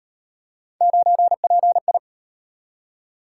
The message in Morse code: 9PI

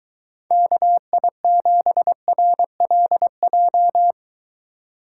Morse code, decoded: KI7RLJ

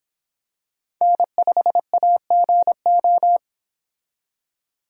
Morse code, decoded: N5AGO

